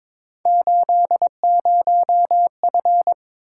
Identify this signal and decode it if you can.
Morse code, heard as a keyed tone, about 22 wpm, 700 Hz. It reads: 80F
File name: practice / 22wpm / Three Letter-Number Combinations - 22wpm